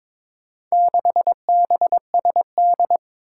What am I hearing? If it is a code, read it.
Morse code, heard as a keyed tone, about 22 wpm, 700 Hz. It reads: 6BSD
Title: Four Challenging Letter-Number Combinations 22wpm